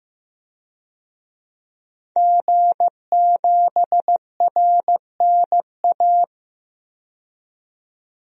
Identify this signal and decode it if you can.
Morse code, heard as a keyed tone, about 16 wpm, 700 Hz. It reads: G7RNA
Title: DXCC-100 Call Signs - 5 Characters - 15wpm